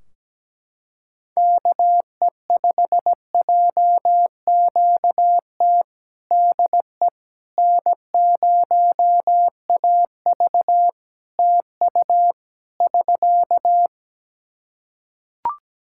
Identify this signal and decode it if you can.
Morse code, heard as a keyed tone, about 17 wpm, 700 Hz. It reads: KE5JQT DE N0AV TU <SK>
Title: Practice QSO - Exchange 4 - Ending - 17wpm